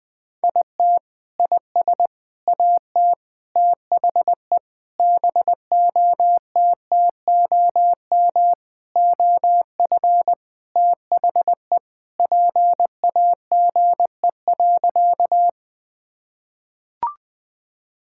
Morse code, heard as a keyed tone, about 20 wpm, 700 Hz. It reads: IT IS AT THE BOTTOM OF THE PAGE.
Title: Sentences with 4-Word Phrases - 20wpm